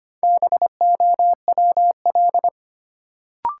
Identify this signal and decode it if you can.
Morse code, heard as a keyed tone, about 25 wpm, 700 Hz. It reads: BOWL
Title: Sentences from Top 1200 Words - New Word Drill - 25wpm